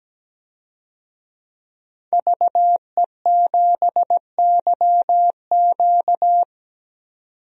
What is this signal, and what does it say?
Morse code, heard as a keyed tone, about 17 wpm, 700 Hz. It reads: VE7YQ
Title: DXCC-100 Call Signs - 5 Characters - 17wpm